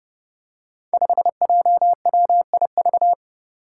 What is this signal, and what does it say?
Morse code, heard as a keyed tone, about 30 wpm, 700 Hz. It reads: 5JWIV
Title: Five Challenging Letter-Number Combinations 30wpm